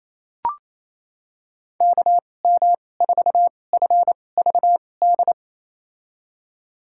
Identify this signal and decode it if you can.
Morse code, heard as a keyed tone, about 28 wpm, spaced as for 20 wpm, 690 Hz. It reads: KM4FVD
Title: USA Call Signs - 6 Characters - 2x Character Spacing - 28wpm